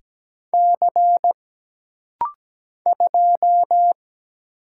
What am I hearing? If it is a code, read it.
Morse code, heard as a keyed tone, about 17 wpm, 700 Hz. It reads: C 2
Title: Single Letter-Number - Rapid-Fire 17wpm